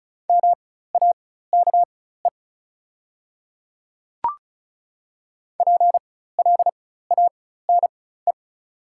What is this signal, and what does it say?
Morse code, heard as a keyed tone, about 36 wpm, spaced as for 16 wpm, 690 Hz. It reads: MAKE PLANE
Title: 4x Character Spacing - Top 500 Words - 35wpm